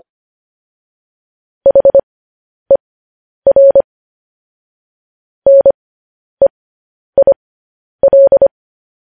Morse code, heard as a keyed tone, about 25 wpm, spaced as for 10 wpm, 550 Hz. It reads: HER NEIL
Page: news headlines in morse code at 10 wpm